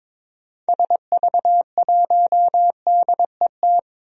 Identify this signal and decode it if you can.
Morse code, heard as a keyed tone, about 22 wpm, 700 Hz. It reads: SV1DET
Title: DXCC-100 Call Signs - 6 Characters - Repeated 3x - 22wpm